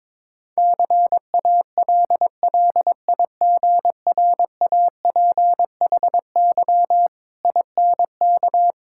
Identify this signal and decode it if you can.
Morse code, heard as a keyed tone, about 22 wpm, 690 Hz. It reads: CALLIGRAPHY INK
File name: Difficult Words with Hints - 22wpm